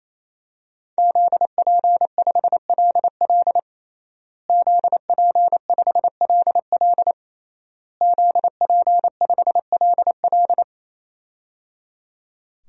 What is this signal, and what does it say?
Morse code, heard as a keyed tone, about 27 wpm, 700 Hz. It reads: ZP5LL ZP5LL ZP5LL
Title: DXCC-100 Call Signs - 5 Characters - Repeated 3x - 28wpm